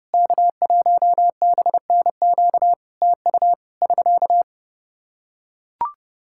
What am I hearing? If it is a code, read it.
Morse code, heard as a keyed tone, about 30 wpm, 700 Hz. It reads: K1BNQ TU <SK>
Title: Practice QSO - Exchange 4 - Ending - 30wpm